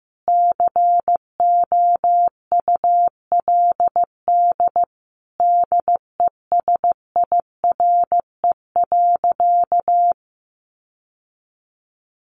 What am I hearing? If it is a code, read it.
Morse code, heard as a keyed tone, about 15 wpm, 700 Hz. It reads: COULD DESIRE.
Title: Sentences from Top 1100 Words 15wpm